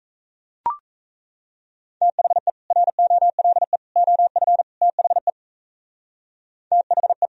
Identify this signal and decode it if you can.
Morse code, heard as a keyed tone, about 36 wpm, 700 Hz. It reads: THE ROLE OF THE THE